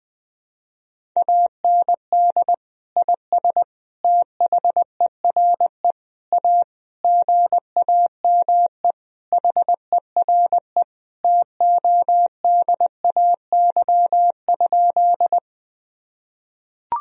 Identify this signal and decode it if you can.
Morse code, heard as a keyed tone, about 21 wpm, 700 Hz. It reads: AND IS THERE A GAME HERE TODAY?
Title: Sentences from Top 1000 Words - Review Part 3 - 20wpm